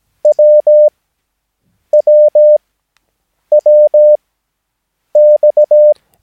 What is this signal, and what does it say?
Morse code, heard as a keyed tone, about 17 wpm, spaced as for 7 wpm, 600 Hz. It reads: WWWX
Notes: Övningsmaterial med alfabetet A till Ö samt siffror 0 till 9 enligt International/Continental Morse Code.